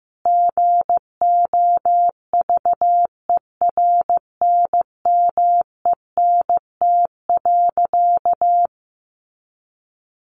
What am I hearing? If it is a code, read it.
Morse code, heard as a keyed tone, about 15 wpm, 700 Hz. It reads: GOVERNMENT.
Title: A Sound Of Thunder 15wpm